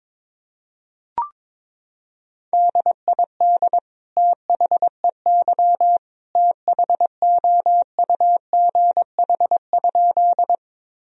Morse code, heard as a keyed tone, about 22 wpm, 700 Hz. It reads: DID THEY THOUGH?
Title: Sentences from Top 1000 Words - Review Part 2 - 22wpm